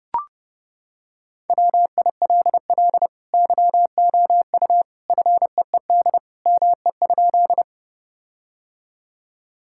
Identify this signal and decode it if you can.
Morse code, heard as a keyed tone, about 30 wpm, 700 Hz. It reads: WILL YOU FEED ME?
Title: Sentences from Top 1600 Words 30wpm